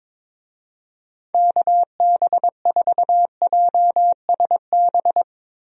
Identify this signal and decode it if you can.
Morse code, heard as a keyed tone, about 22 wpm, 700 Hz. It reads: KB4JSB
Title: USA Call Signs - 6 Characters - 22wpm